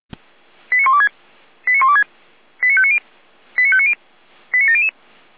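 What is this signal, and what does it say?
Five-tone selective call (ZVEI), heard as quick runs of five tones, tones of about 70 ms. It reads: 89126 89126 89590 89590 89700
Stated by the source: ZVEI five-tone selective call received on SDR.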